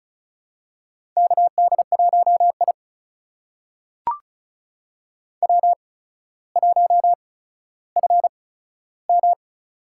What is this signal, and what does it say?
Morse code, heard as a keyed tone, about 36 wpm, spaced as for 15 wpm, 700 Hz. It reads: KD1I W1FM
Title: USA Call Signs - 4 Characters - 8x Character Spacing - 35wpm